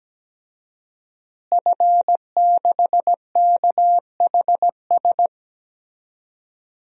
Morse code, heard as a keyed tone, about 17 wpm, 700 Hz. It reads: F6KHS
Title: DXCC-100 Call Signs - 5 Characters - Repeated 2x - 17wpm